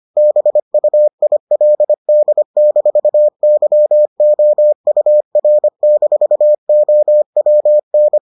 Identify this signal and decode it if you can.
Morse code, heard as a keyed tone, about 25 wpm, 600 Hz. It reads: BUILD-YOUR-OWN